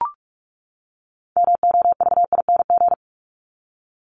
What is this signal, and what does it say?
Morse code, heard as a keyed tone, about 44 wpm, 700 Hz. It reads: MOVING
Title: ING Suffix - Top 100 Words with Base Words 45wpm